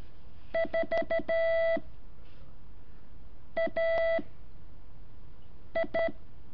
Morse code, heard as a keyed tone, about 13 wpm, 660 Hz. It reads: 4 A I
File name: curso / ejercicio 7